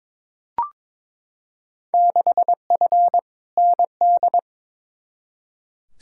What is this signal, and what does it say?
Morse code, heard as a keyed tone, about 22 wpm, 700 Hz. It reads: 6F ND
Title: Field Day Exchanges Encore - 22wpm